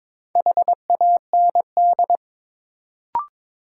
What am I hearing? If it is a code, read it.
Morse code, heard as a keyed tone, about 21 wpm, 700 Hz. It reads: HAND
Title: Sets of 3 Words - New Word Drill - 22wpm